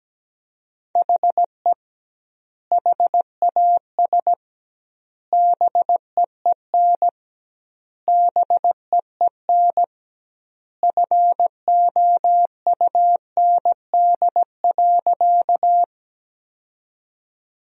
Morse code, heard as a keyed tone, about 17 wpm, 700 Hz. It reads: HE HAS BEEN BEEN FOUND.
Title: 2x Word-Spacing - Sentences from Top 300 Words - 17wpm